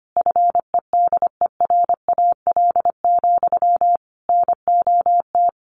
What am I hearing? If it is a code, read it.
Morse code, heard as a keyed tone, about 25 wpm, 700 Hz. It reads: FEDERAL, NOT
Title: Sentences from Top 1200 Words 25wpm